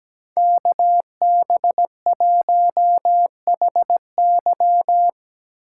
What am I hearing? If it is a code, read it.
Morse code, heard as a keyed tone, about 20 wpm, 700 Hz. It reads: KB1HY
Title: Speed Racing - USA 2x2 Call Signs 17wpm